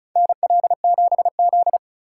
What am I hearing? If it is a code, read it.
Morse code, heard as a keyed tone, about 36 wpm, 700 Hz. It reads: NL7Z
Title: USA Call Signs - 4 Characters - 3x Character Spacing - 35wpm